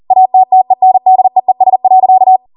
Morse code, heard as a keyed tone, about 40 wpm, 750 Hz. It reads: ATTENDEES.